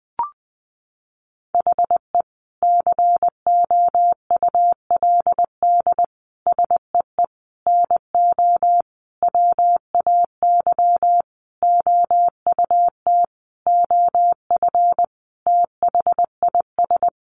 Morse code, heard as a keyed tone, about 20 wpm, 700 Hz. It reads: HE COULD SEE NO WAY OUT OF THIS
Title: Sentences from Top 1200 Words 20wpm